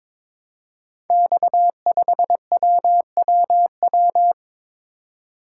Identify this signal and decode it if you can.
Morse code, heard as a keyed tone, about 22 wpm, 700 Hz. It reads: X5WWW
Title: Code Groups - 5 Letter:Numbers - 22wpm